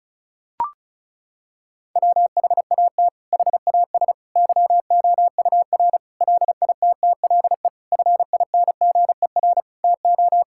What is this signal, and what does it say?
Morse code, heard as a keyed tone, about 36 wpm, 700 Hz. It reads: WHAT HAS YOUR LITTLE FINGER TO
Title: Sentences from Top 1600 Words 35wpm